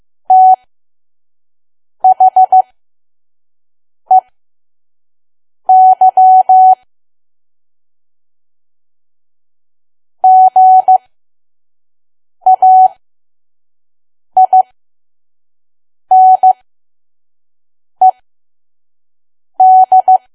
Morse code, heard as a keyed tone, about 15 wpm, spaced as for 5 wpm, 750 Hz. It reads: THEY GAINED
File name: Archive / 5 WPM files / PM